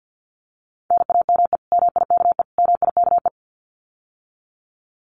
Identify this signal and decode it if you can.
Morse code, heard as a keyed tone, about 50 wpm, 700 Hz. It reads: NAME MIKE MIKE
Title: Common QSO Phrases Encore 50wpm